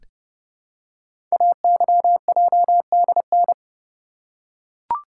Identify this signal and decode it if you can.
Morse code, heard as a keyed tone, about 30 wpm, 700 Hz. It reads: AYJDN
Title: Code Groups - 5 Letters - 30wpm